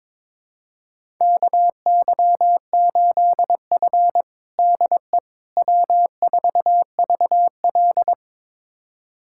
Morse code, heard as a keyed tone, about 22 wpm, 700 Hz. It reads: KY8F DE W4VL
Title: USA Callsign DE Callsign - 4 Characters - 22wpm